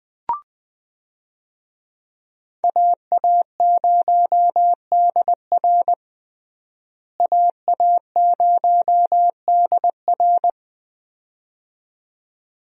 Morse code, heard as a keyed tone, about 20 wpm, 700 Hz. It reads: AA0DR AA0DR
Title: USA Call Signs - 5 Characters - Repeated 2x - 20wpm